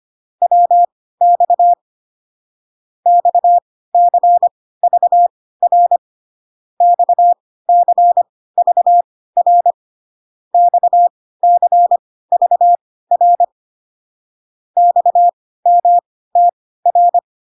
Morse code, heard as a keyed tone, about 25 wpm, spaced as for 16 wpm, 700 Hz. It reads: WX XCVR XCVR XCVR XMTR